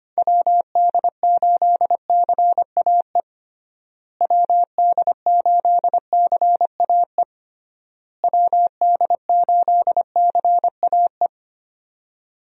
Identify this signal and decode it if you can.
Morse code, heard as a keyed tone, about 25 wpm, 700 Hz. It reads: WD8CAE WD8CAE WD8CAE
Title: USA Call Signs - 6 Characters - Repeated 3x - 25wpm